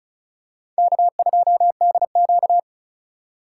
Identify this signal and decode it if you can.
Morse code, heard as a keyed tone, about 34 wpm, 700 Hz. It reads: K2DQ
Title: USA Call Signs - 4x Character Spacing - 35wpm